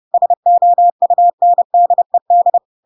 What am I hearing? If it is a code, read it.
Morse code, heard as a keyed tone, about 30 wpm, 700 Hz. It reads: SOUNDED
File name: Pinocchio / Pinocchio ch6 30 30 ik1hge